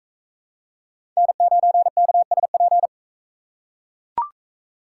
Morse code, heard as a keyed tone, about 42 wpm, 700 Hz. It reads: N9KSP